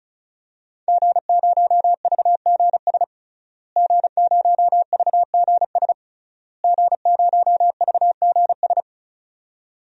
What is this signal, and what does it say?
Morse code, heard as a keyed tone, about 34 wpm, 700 Hz. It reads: G0VGS G0VGS G0VGS
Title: DXCC-100 Call Signs - 5 Characters - Repeated 3x - 35wpm